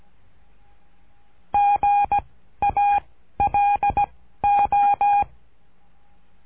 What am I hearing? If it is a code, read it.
Morse code, heard as a keyed tone, about 17 wpm, 810 Hz. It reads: G A L O
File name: cursos / CURSO DE TELEGRAFIA / Telegrafia - Trilha 13